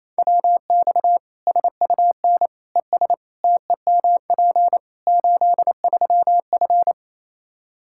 Morse code, heard as a keyed tone, about 27 wpm, 700 Hz. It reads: WX SUN ES TEMP 83F
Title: K1USN SST QSO Practice 28wpm